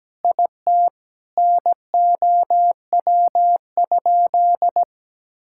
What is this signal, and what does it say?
Morse code, heard as a keyed tone, about 17 wpm, 700 Hz. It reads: IT NOW?
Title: Sentences from Top 1000 Words - Review Part 5 - 17wpm